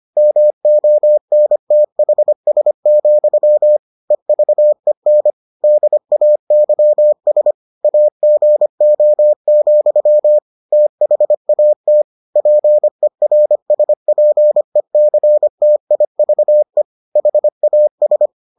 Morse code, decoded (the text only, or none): MONTHS, EVEN DAYS AGO, THAT PERSPECTIVE HAS